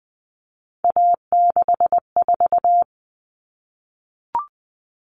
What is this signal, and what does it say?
Morse code, heard as a keyed tone, about 20 wpm, 700 Hz. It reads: A64
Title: All Cut Numbers - 3-Digit 20wpm